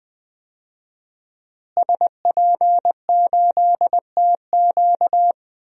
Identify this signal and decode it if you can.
Morse code, heard as a keyed tone, about 20 wpm, 700 Hz. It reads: SP8TQ
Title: DXCC-100 Call Signs - 5 Characters - Repeated 2x - 20wpm